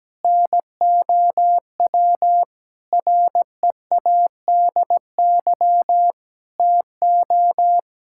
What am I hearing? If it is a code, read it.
Morse code, heard as a keyed tone, about 17 wpm, 700 Hz. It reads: NOW READY TO
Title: Sentences from Top 1600 Words 17wpm